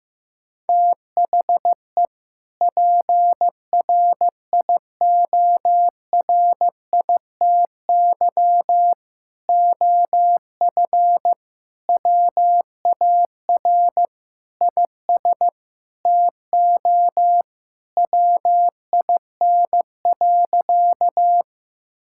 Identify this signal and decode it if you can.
Morse code, heard as a keyed tone, about 15 wpm, 700 Hz. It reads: THE PRIORITY OF WAR IS TO WIN.
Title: Sentences from Top 1600 Words 15wpm